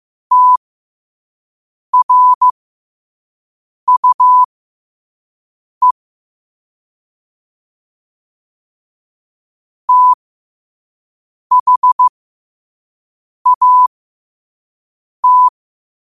Morse code, heard as a keyed tone, about 15 wpm, spaced as for 5 wpm, 1000 Hz. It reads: TRUE THAT
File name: qotdfiles / 5wpm